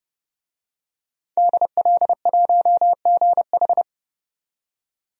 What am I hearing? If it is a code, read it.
Morse code, heard as a keyed tone, about 30 wpm, 700 Hz. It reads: DL1GH
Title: DXCC-100 Call Signs - 5 Characters - 30wpm